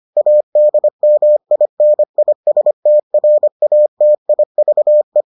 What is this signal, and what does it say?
Morse code, heard as a keyed tone, about 25 wpm, 600 Hz. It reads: ADMINISTRATIVE